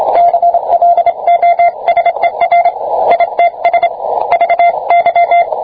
Morse code, heard as a keyed tone, about 26 wpm, 690 Hz, with radio noise. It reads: CLOSER ITS VY